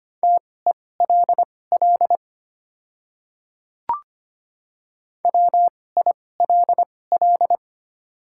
Morse code, heard as a keyed tone, about 25 wpm, spaced as for 18 wpm, 700 Hz. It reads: TELL WILL
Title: 2x Character Spacing - Top 300 Words - 25wpm